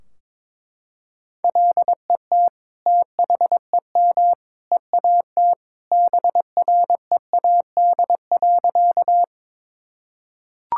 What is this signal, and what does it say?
Morse code, heard as a keyed tone, about 22 wpm, 700 Hz. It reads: LET THEM EAT BREAD.
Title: Sentences from Top 1600 Words 22wpm